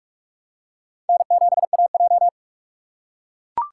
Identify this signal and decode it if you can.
Morse code, heard as a keyed tone, about 43 wpm, 700 Hz. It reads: N7AJ
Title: USA Call Signs - 4 Characters - 3x Character Spacing - 45wpm